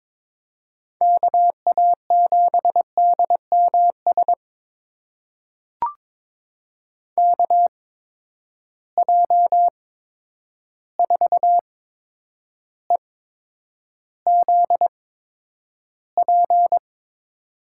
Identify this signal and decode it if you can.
Morse code, heard as a keyed tone, about 22 wpm, spaced as for 9 wpm, 700 Hz. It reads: KA7DMS KJ4EZP
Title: USA Call Signs - 6 Characters - 8x Character Spacing - 22wpm